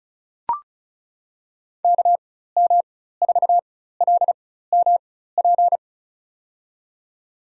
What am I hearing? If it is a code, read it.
Morse code, heard as a keyed tone, about 34 wpm, spaced as for 16 wpm, 700 Hz. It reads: KM4LMP
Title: USA Call Signs - 6 Characters - 4x Character Spacing - 35wpm